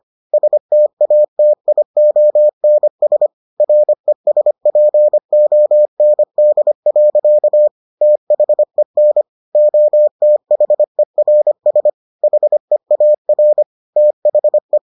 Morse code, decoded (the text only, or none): STATIONS RESPOND. THEN OTHERS HEAR THE